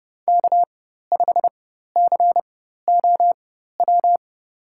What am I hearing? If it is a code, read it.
Morse code, heard as a keyed tone, about 30 wpm, spaced as for 14 wpm, 700 Hz. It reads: K5COW
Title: USA Call Signs - 5 Characters - 4x Character Spacing - 30wpm